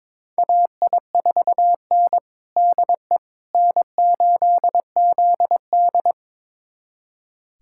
Morse code, heard as a keyed tone, about 22 wpm, 700 Hz. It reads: AI4N DE N8ZD
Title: USA Callsign DE Callsign - 4 Characters - 22wpm